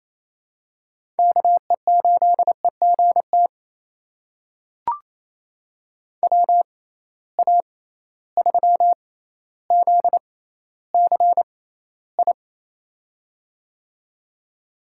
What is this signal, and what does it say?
Morse code, heard as a keyed tone, about 28 wpm, spaced as for 14 wpm, 700 Hz. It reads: KE8EGT WA3ZCI
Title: USA Call Signs - 6 Characters - 6x Character Spacing - 28wpm